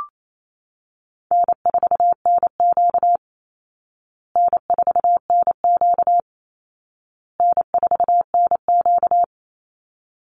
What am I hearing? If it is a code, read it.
Morse code, heard as a keyed tone, about 28 wpm, 700 Hz. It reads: N4NQ N4NQ N4NQ
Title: Call Signs Repeated 2x - 28wpm